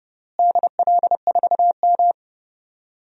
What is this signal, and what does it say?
Morse code, heard as a keyed tone, about 30 wpm, 700 Hz. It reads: DL4M
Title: DXCC-100 Call Signs - 4 Characters - Repeated 2x - 30wpm